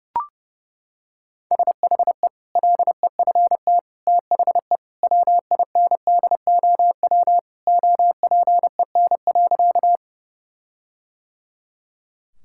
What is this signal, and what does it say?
Morse code, heard as a keyed tone, about 30 wpm, 700 Hz. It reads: SHE LEFT THE WINDOW OPEN.
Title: Sentences from Top 1600 Words 30wpm